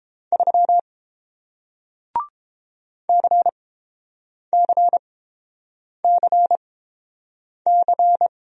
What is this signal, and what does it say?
Morse code, heard as a keyed tone, about 30 wpm, spaced as for 7 wpm, 700 Hz. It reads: 3 CCCC